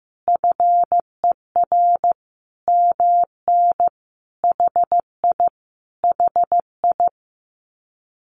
Morse code, heard as a keyed tone, about 15 wpm, 700 Hz. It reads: FER MN HI HI
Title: Common QSO Phrases Encore 15wpm